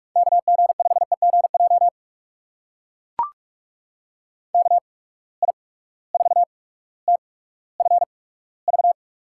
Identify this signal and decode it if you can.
Morse code, heard as a keyed tone, about 44 wpm, 690 Hz. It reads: K G 5 E G J K I 4 T F V
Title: USA Call Signs - 8x Character Spacing - 45wpm